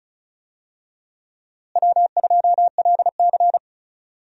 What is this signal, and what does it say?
Morse code, heard as a keyed tone, about 36 wpm, 700 Hz. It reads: W2LC
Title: USA Call Signs - 4 Characters - Repeated 2x - 35wpm